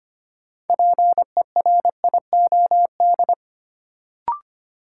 Morse code, heard as a keyed tone, about 25 wpm, 700 Hz. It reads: PERIOD
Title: Sentences from Top 800 Words - New Word Drill - 25wpm